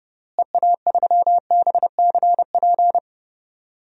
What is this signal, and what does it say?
Morse code, heard as a keyed tone, about 31 wpm, 700 Hz. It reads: EA3BCP
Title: DXCC-100 Call Signs - 6 Characters - Repeated 3x - 30wpm